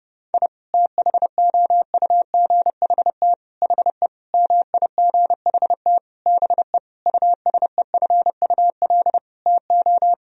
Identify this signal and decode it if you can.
Morse code, heard as a keyed tone, about 30 wpm, 700 Hz. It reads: I THOUGHT HE MIGHT BE USEFUL TO
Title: Sentences from Top 1000 Words - Review Part 4 - 30wpm